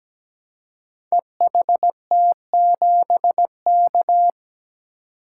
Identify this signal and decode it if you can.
Morse code, heard as a keyed tone, about 17 wpm, 700 Hz. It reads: EHT7K
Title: Code Groups - 5 Letter:Numbers - 17wpm